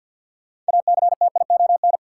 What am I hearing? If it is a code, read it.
Morse code, heard as a keyed tone, about 52 wpm, 700 Hz. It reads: ACTION
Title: Sentences from Top 700 Words - New Word Drill - 50wpm